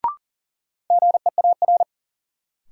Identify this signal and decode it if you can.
Morse code, heard as a keyed tone, about 38 wpm, 700 Hz. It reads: GEAR
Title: Sentences from Top 1400 Words - New Word Drill Alt - 40wpm